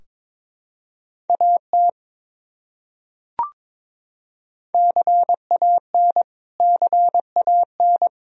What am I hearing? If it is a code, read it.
Morse code, heard as a keyed tone, about 22 wpm, 700 Hz. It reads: AT CAN CAN
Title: Top 100 Words Repeated 22wpm